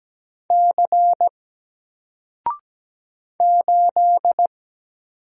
Morse code, heard as a keyed tone, about 17 wpm, 700 Hz. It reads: C 8
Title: Single Character (Letter:Number:Punctuation:Prosign) Alt 17wpm